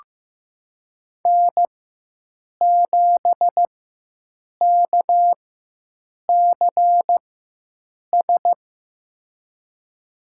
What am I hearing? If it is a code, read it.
Morse code, heard as a keyed tone, about 15 wpm, spaced as for 7 wpm, 700 Hz. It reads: N7KCS
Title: USA Call Signs - 5 Characters - 4x Character Spacing - 15wpm